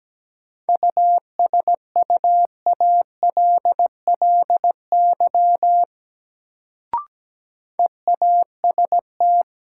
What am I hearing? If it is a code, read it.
Morse code, heard as a keyed tone, about 17 wpm, 700 Hz. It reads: USUALLY EAST
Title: Sentences from Top 1000 Words - Review Part 3 - New Word Drill - 17wpm